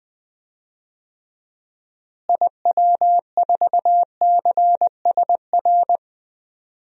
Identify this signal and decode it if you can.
Morse code, heard as a keyed tone, about 20 wpm, 700 Hz. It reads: IW4CSR
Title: DXCC-100 Call Signs - 6 Characters - Repeated 3x - 20wpm